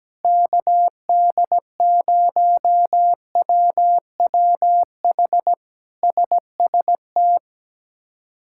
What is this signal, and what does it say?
Morse code, heard as a keyed tone, about 17 wpm, 700 Hz. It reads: KD0WWH SST
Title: K1USN SST QSO Practice - 17wpm